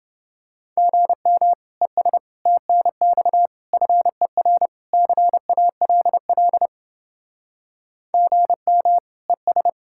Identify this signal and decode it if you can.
Morse code, heard as a keyed tone, about 30 wpm, 700 Hz. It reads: GM ES TNX FER CALL GM ES